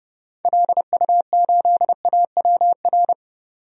Morse code, heard as a keyed tone, about 30 wpm, 700 Hz. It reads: LU8AWR
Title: DXCC-100 Call Signs - 6 Characters - 30wpm